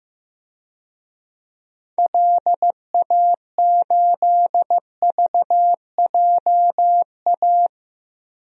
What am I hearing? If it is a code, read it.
Morse code, heard as a keyed tone, about 15 wpm, 700 Hz. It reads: LA8VJA
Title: DXCC-100 Call Signs - 6 Characters - Repeated 3x - 15wpm